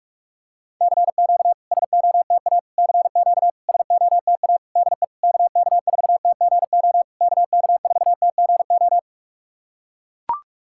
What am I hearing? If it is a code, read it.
Morse code, heard as a keyed tone, about 42 wpm, 690 Hz. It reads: CQ SOTA CQ SOTA DE KK4TGO KK4TGO
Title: SOTA QSO Practice 45wpm